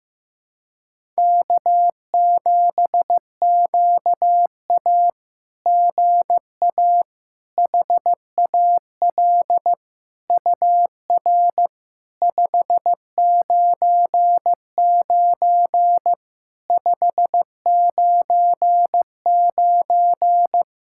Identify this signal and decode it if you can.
Morse code, heard as a keyed tone, about 15 wpm, 700 Hz. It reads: K7QA GA HAL UR 599 599
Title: SOTA QSO Practice 15wpm